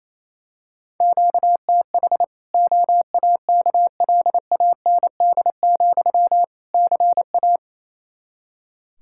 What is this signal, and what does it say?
Morse code, heard as a keyed tone, about 27 wpm, 700 Hz. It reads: QTH OAKLAND, CA
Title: K1USN SST QSO Practice 28wpm